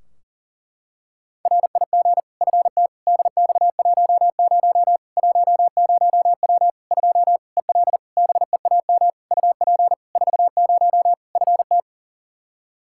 Morse code, heard as a keyed tone, about 40 wpm, 700 Hz. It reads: RIG FT DX10 10W 2 EL BEAM UP 40 FT
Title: Practice QSO - Exchange 2 - Info - 40wpm